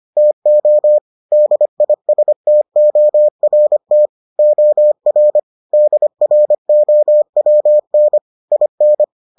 Morse code, heard as a keyed tone, about 25 wpm, 600 Hz. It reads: TO DISTORT OR DROWN IN